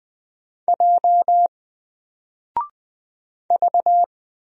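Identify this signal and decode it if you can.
Morse code, heard as a keyed tone, about 20 wpm, 700 Hz. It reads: J V
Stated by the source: Single letter and number Morse drill at 20 wpm, from N4LCD.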